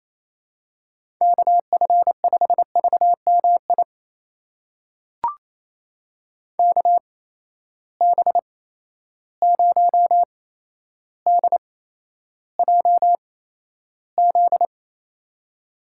Morse code, heard as a keyed tone, about 28 wpm, spaced as for 12 wpm, 700 Hz. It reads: KF5VMI KB0DJZ
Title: USA Call Signs - 6 Characters - 8x Character Spacing - 28wpm